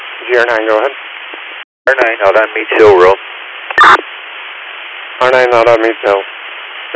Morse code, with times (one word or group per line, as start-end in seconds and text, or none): none